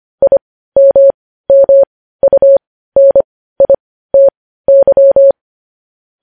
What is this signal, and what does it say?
Morse code, heard as a keyed tone, about 25 wpm, spaced as for 15 wpm, 550 Hz. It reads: IMMUNITY